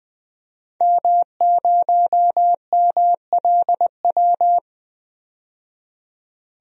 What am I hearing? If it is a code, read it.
Morse code, heard as a keyed tone, about 20 wpm, 700 Hz. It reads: M0MLW
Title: DXCC-100 Call Signs - 5 Characters - 20wpm